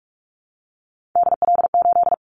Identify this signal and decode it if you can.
Morse code, heard as a keyed tone, about 44 wpm, 700 Hz. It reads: DL8
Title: Callsign Prefixes for Top 25 DXCC - 45wpm